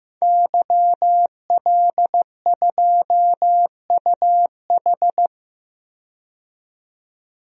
Morse code, heard as a keyed tone, about 15 wpm, 700 Hz. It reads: YL2UH
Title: DXCC-100 Call Signs - 5 Characters - 15wpm